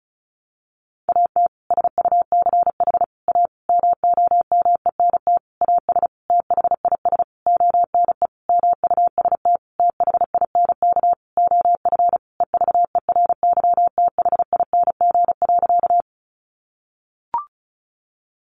Morse code, heard as a keyed tone, about 34 wpm, 700 Hz. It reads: AT SUCH A MOMENT AS THIS ONE MUST THINK OF EVERYTHING.
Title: Sentences from Top 1000 Words 35wpm